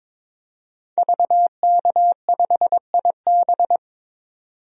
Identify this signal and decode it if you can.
Morse code, heard as a keyed tone, about 22 wpm, 700 Hz. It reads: VK5IB
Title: DXCC-100 Call Signs - 5 Characters - 22wpm